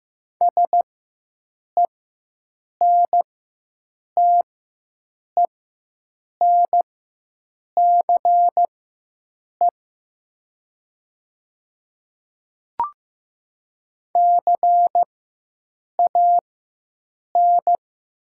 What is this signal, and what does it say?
Morse code, heard as a keyed tone, about 15 wpm, spaced as for 7 wpm, 700 Hz. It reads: SENTENCE CAN
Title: 4x Character Spacing - Top 200 Words - 15wpm